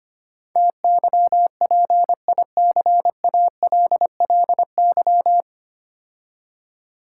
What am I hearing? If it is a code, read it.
Morse code, heard as a keyed tone, about 25 wpm, 700 Hz. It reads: TYPICALLY
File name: practice / 25wpm / 9 Letter Words - First 3 Letters Given - 25wpm